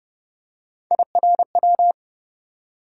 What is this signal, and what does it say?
Morse code, heard as a keyed tone, about 32 wpm, 700 Hz. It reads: IRW
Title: USA Call Sign Postfixes - 3 Letters 30wpm